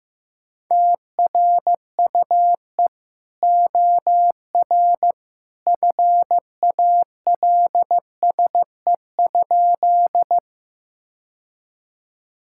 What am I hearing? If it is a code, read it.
Morse code, heard as a keyed tone, about 15 wpm, 700 Hz. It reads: TRUE OR FALSE?
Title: Sentences from Top 1600 Words 15wpm